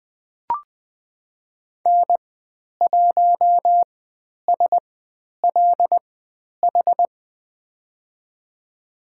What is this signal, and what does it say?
Morse code, heard as a keyed tone, about 20 wpm, spaced as for 10 wpm, 700 Hz. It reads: N1SLH
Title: 095 Lesson 16g - Letter c - Call Signs 20wpm